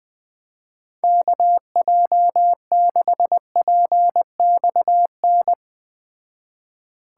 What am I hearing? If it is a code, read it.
Morse code, heard as a keyed tone, about 19 wpm, 700 Hz. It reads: KJ6PXN
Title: Speed Racing - USA 2x3 Call Signs 17wpm